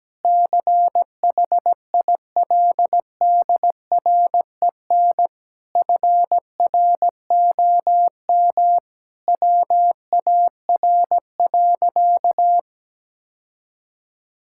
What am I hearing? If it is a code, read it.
Morse code, heard as a keyed tone, about 17 wpm, 700 Hz. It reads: CHILDREN FROM WAR.
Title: Sentences from Top 1000 Words - Review Part 6 - 17wpm